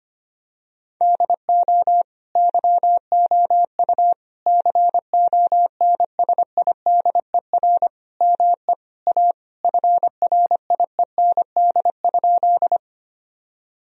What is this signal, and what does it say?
Morse code, heard as a keyed tone, about 25 wpm, 700 Hz. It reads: DO YOU CONSIDER ME A FRIEND?
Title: Sentences from Top 1000 Words - Review Part 4 - 25wpm